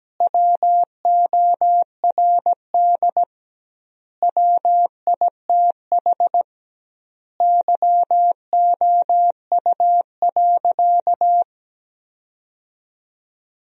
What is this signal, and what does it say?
Morse code, heard as a keyed tone, about 17 wpm, 700 Hz. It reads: WORD WITH YOU.
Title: 2x Word-Spacing - Sentences from Top 300 Words - 17wpm